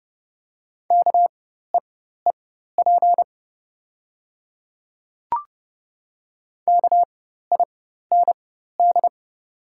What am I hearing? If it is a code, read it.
Morse code, heard as a keyed tone, about 30 wpm, spaced as for 14 wpm, 700 Hz. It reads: KEEP KIND
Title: 4x Character Spacing - Top 500 Words - 30wpm